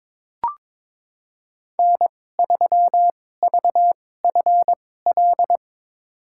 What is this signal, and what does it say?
Morse code, heard as a keyed tone, about 21 wpm, spaced as for 16 wpm, 700 Hz. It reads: N3VFL
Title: USA Call Signs - 5 Characters - 2x Character Spacing - 22wpm